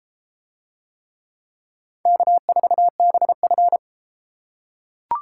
Morse code, heard as a keyed tone, about 31 wpm, 700 Hz. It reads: K4BF